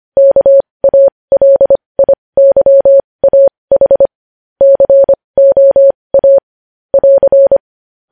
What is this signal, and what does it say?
Morse code, heard as a keyed tone, about 25 wpm, spaced as for 20 wpm, 550 Hz. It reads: KALIYAH COA <AR>
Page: news-headlines-in-morse-code-at-20-wpm